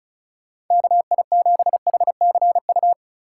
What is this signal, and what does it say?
Morse code, heard as a keyed tone, about 36 wpm, 700 Hz. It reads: KI7HCU
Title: USA Call Signs - 6 Characters - 3x Character Spacing - 35wpm